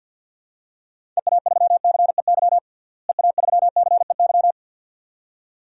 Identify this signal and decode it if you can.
Morse code, heard as a keyed tone, about 50 wpm, 700 Hz. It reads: EA3CEY EA3CEY
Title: DXCC-100 Call Signs - 6 Characters - Repeated 2x - 50wpm